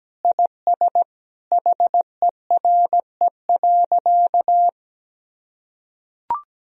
Morse code, heard as a keyed tone, about 17 wpm, 700 Hz. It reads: IS HERE.
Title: Sentences from Top 1000 Words - Review Part 4 - 17wpm